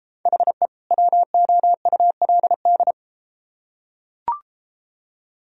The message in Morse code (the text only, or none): HE WOULD